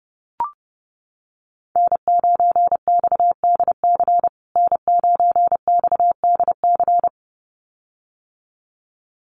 Morse code, heard as a keyed tone, about 30 wpm, 700 Hz. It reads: N9XDC N9XDC
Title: USA Call Signs Repeated 2x 30wpm